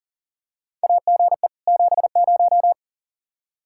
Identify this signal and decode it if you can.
Morse code, heard as a keyed tone, about 40 wpm, 700 Hz. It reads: AGE 70
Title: Practice QSO - Exchange 3 - More Info - 40wpm